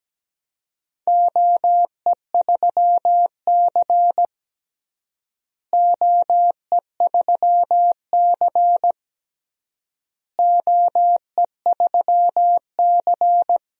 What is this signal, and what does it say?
Morse code, heard as a keyed tone, about 17 wpm, 700 Hz. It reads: OE3C OE3C OE3C
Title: DXCC-100 Call Signs - 4 Characters - Repeated 3x - 17wpm